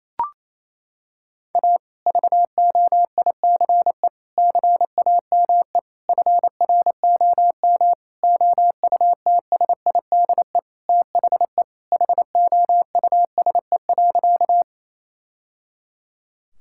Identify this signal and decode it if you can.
Morse code, heard as a keyed tone, about 28 wpm, 700 Hz. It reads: A VOICE CAME FROM OUTSIDE THE HOUSE.
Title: Sentences from Top 1000 Words - Review Part 6 - 28wpm